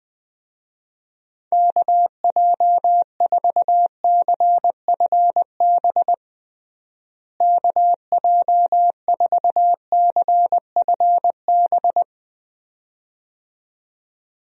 Morse code, heard as a keyed tone, about 20 wpm, 700 Hz. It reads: KJ4CFB KJ4CFB
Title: USA Call Signs - 6 Characters - Repeated 2x - 20wpm